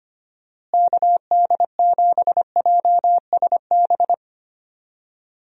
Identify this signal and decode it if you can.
Morse code, heard as a keyed tone, about 25 wpm, 700 Hz. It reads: KD7JSB
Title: USA Call Signs - 6 Characters - Repeated 3x - 25wpm